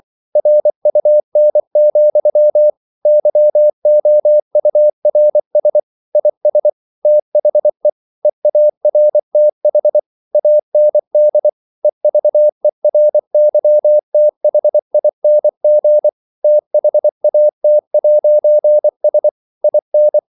Morse code, heard as a keyed tone, about 24 wpm, 600 Hz. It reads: RUN, YOURS IS THE EARTH AND EVERYTHING THAT'S IN
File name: CW / Practice / If24